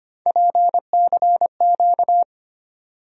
Morse code, heard as a keyed tone, about 25 wpm, 700 Hz. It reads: PCQ
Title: USA Call Sign Postfixes - 3 Letters 25wpm